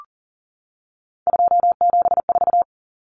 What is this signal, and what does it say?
Morse code, heard as a keyed tone, about 40 wpm, 700 Hz. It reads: 274
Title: Numbers - 3-Digit 40wpm